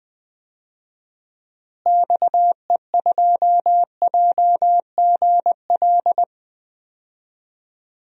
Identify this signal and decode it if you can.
Morse code, heard as a keyed tone, about 20 wpm, 700 Hz. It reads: XE2JGL
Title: DXCC-100 Call Signs - 6 Characters - Repeated 3x - 20wpm